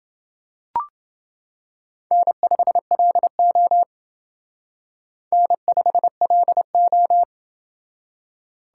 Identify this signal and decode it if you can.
Morse code, heard as a keyed tone, about 28 wpm, 700 Hz. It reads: N5LO N5LO